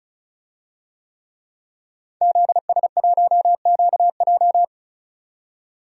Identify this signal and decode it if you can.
Morse code, heard as a keyed tone, about 36 wpm, 700 Hz. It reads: ZS1QJ
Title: DXCC-100 Call Signs - 5 Characters - 35wpm